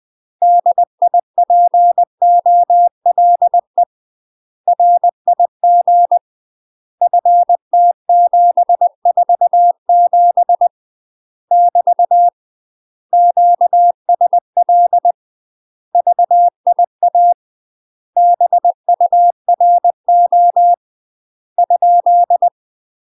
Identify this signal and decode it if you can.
Morse code, heard as a keyed tone, about 20 wpm, 700 Hz. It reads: DIPOLE RIG FT747 <BT> QSL VIA BURO ?
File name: UK MASTER QSOs TRAINING / 20WPM / UK MASTER QSOs 0039